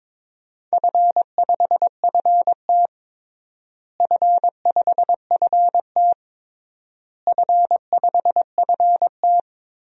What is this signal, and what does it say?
Morse code, heard as a keyed tone, about 22 wpm, 700 Hz. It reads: F5FT F5FT F5FT
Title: DXCC-100 Call Signs - 4 Characters - Repeated 3x - 22wpm